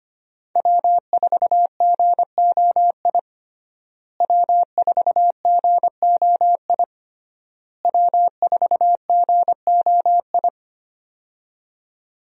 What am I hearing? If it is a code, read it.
Morse code, heard as a keyed tone, about 25 wpm, 700 Hz. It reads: W4GOI W4GOI W4GOI
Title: USA Call Signs - 5 Characters - Repeated 3x - 25wpm